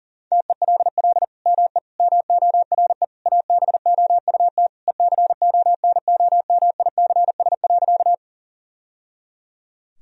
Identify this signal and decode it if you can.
Morse code, heard as a keyed tone, about 40 wpm, 700 Hz. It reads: TELL ME MORE ABOUT ECONOMICS.
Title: Sentences from Top 1000 Words - Review Part 3 - 40wpm